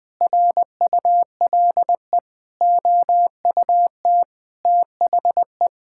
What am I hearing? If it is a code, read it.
Morse code, heard as a keyed tone, about 20 wpm, 700 Hz. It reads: RULE OUT THE
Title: Sentences with 8-Word Phrases - 20wpm